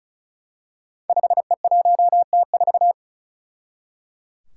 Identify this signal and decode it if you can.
Morse code, heard as a keyed tone, about 36 wpm, 700 Hz. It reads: 5E1T4
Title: Five Challenging Letter-Number Combinations 35wpm